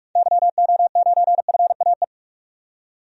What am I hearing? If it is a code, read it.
Morse code, heard as a keyed tone, about 42 wpm, 700 Hz. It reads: YO9FAE
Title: DXCC-100 Call Signs - 6 Characters - 45wpm